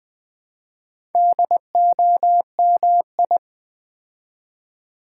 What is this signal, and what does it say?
Morse code, heard as a keyed tone, about 20 wpm, 700 Hz. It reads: DOMI
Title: Build A Word - 4 Letters At A Time - 8-Letter Words - 20wpm